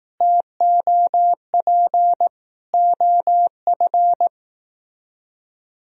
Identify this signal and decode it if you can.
Morse code, heard as a keyed tone, about 18 wpm, 700 Hz. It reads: TOP OF
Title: Speed-Racing - Sets of 3 Words 15wpm